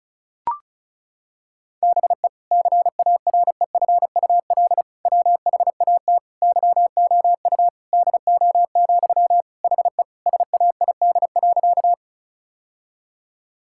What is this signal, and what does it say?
Morse code, heard as a keyed tone, about 36 wpm, 700 Hz. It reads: BE CAREFUL WHAT YOU DO, HE SAID.
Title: Sentences from Top 1600 Words 35wpm